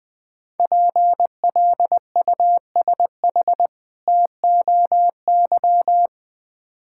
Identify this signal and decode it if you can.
Morse code, heard as a keyed tone, about 21 wpm, 700 Hz. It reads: PLUSH TOY
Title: Difficult Words with Hints - 20wpm